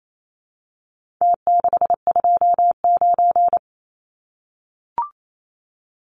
Morse code, heard as a keyed tone, about 28 wpm, 700 Hz. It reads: T629
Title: Numbers - Contest Style - 4-Digit 28wpm